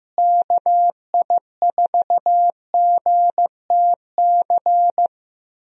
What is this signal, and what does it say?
Morse code, heard as a keyed tone, about 15 wpm, 700 Hz. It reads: KI4GTC
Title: USA Call Signs - 6 Characters - Repeated 2x - 15wpm